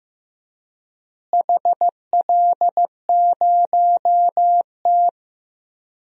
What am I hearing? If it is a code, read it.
Morse code, heard as a keyed tone, about 15 wpm, 700 Hz. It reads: HL0T
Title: DXCC-100 Call Signs - 4 Characters - Repeated 3x - 15wpm